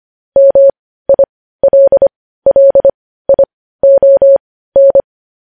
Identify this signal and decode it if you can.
Morse code, heard as a keyed tone, about 25 wpm, spaced as for 15 wpm, 550 Hz. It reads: MILLION
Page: news-headlines-in-morse-code-at-15-wpm